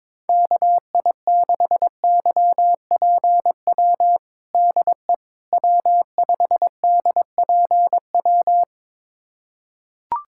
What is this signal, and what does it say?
Morse code, heard as a keyed tone, about 21 wpm, 700 Hz. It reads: KI6YPW DE W5DPW
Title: USA Callsign DE Callsign 22wpm